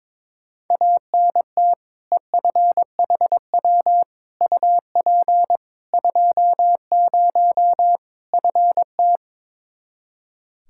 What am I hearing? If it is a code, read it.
Morse code, heard as a keyed tone, about 22 wpm, 700 Hz. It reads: ANT EFHW UP 20 FT
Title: Practice QSO - Exchange 2 - Info - 22wpm